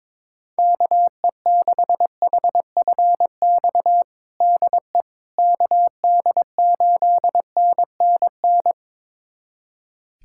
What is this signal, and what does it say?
Morse code, heard as a keyed tone, about 22 wpm, 700 Hz. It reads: KE6HFX DE KD8NNN
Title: USA Callsign DE Callsign - 6 Characters - 22wpm